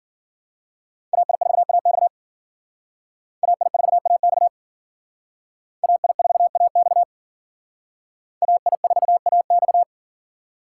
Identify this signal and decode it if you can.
Morse code, heard as a keyed tone, about 52 wpm, 700 Hz. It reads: AI4AX AI4AX AI4AX AI4AX